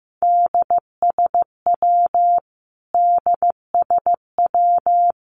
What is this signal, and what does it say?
Morse code, heard as a keyed tone, about 15 wpm, 700 Hz. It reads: DSW DSW
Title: Abbreviations Repeated 3x - 15wpm